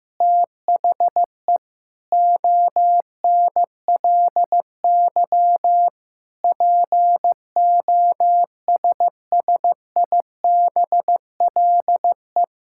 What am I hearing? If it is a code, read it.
Morse code, heard as a keyed tone, about 15 wpm, 700 Hz. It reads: THE ONLY POSSIBLE
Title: Sentences from Top 1000 Words - Review Part 5 - 15wpm